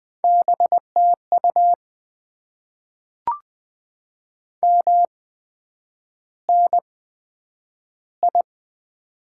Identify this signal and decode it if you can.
Morse code, heard as a keyed tone, about 20 wpm, spaced as for 9 wpm, 700 Hz. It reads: BTU MNI
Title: Top 90 QSO Elements - 8x Character Spacing - 20wpm